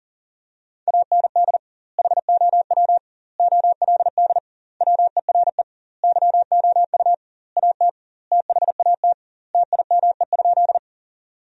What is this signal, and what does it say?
Morse code, heard as a keyed tone, about 40 wpm, 700 Hz. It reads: AND HOW OLD WERE YOU AT THAT TIME?
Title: 2x Word-Spacing - Sentences from Top 200 Words - 40wpm